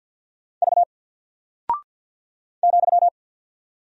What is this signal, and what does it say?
Morse code, heard as a keyed tone, about 48 wpm, 700 Hz. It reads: V ,
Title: Single Character (Letter:Number:Punctuation:Prosign) 50wpm